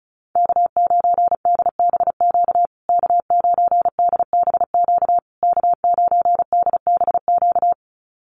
Morse code, heard as a keyed tone, about 34 wpm, 700 Hz. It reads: K9DBQ K9DBQ K9DBQ
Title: USA Call Signs Repeated 3x 35wpm